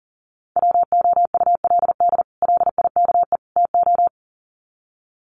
Morse code, heard as a keyed tone, about 40 wpm, 700 Hz. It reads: WOULD LIKE TO
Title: Sets of 3 Words 40wpm